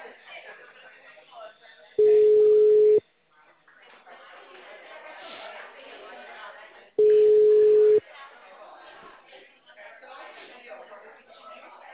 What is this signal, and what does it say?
Ringback tone (Europe, 425 Hz), with speech.